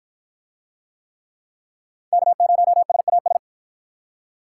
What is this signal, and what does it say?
Morse code, heard as a keyed tone, about 52 wpm, 700 Hz. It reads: K0SAS